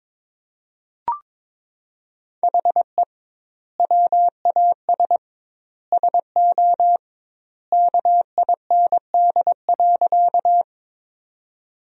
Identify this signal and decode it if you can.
Morse code, heard as a keyed tone, about 22 wpm, 700 Hz. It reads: HE WAS SO KIND.
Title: 2x Word-Spacing - Sentences from Top 200 Words - 22wpm